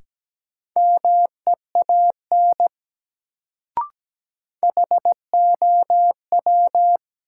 Morse code, heard as a keyed tone, about 17 wpm, 700 Hz. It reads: MEAN HOW
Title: Sets of 2 Words - New Word Drill - 17wpm